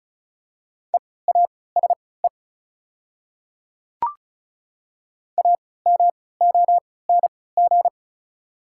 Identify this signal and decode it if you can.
Morse code, heard as a keyed tone, about 36 wpm, spaced as for 20 wpm, 690 Hz. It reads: EASE AMONG
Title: 3x Character Spacing - Top 500 Words - 35wpm